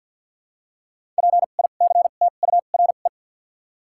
Morse code, heard as a keyed tone, about 53 wpm, spaced as for 37 wpm, 700 Hz. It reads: PICTURE
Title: 2x Character Spacing - Top 200 Words - 50wpm